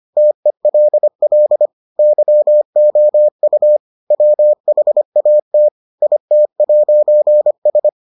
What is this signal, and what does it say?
Morse code, heard as a keyed tone, about 25 wpm, 600 Hz. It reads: TELL YOU WHAT IT'S